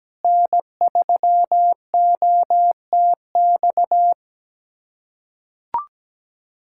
Morse code, heard as a keyed tone, about 17 wpm, 700 Hz. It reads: N3OTX
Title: USA Callsign DE Callsign - 5 Characters - 17wpm